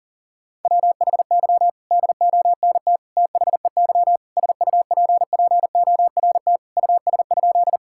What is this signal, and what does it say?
Morse code, heard as a keyed tone, about 40 wpm, 700 Hz. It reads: WHY DONT THEY SUPPORT US?
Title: Sentences from Top 1000 Words - Review Part 5 - 40wpm